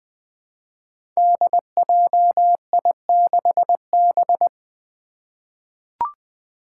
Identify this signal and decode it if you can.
Morse code, heard as a keyed tone, about 20 wpm, 700 Hz. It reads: DJI6B
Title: Five Challenging Letter-Number Combinations 20wpm